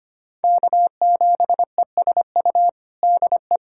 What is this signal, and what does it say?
Morse code, heard as a keyed tone, about 25 wpm, 700 Hz. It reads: K7ESU DE
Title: USA Callsign DE Callsign - 5 Characters - 25wpm